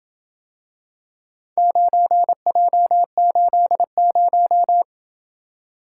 Morse code, heard as a keyed tone, about 27 wpm, 700 Hz. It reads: 9J80